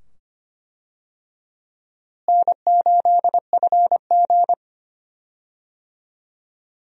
Morse code, heard as a keyed tone, about 25 wpm, 700 Hz. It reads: N8FG
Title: USA Call Signs - 4 Characters - Repeated 2x - 25wpm